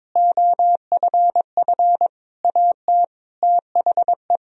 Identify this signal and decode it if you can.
Morse code, heard as a keyed tone, about 22 wpm, 700 Hz. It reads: OFF AT THE
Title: Sentences from Top 1000 Words - Review Part 3 - 22wpm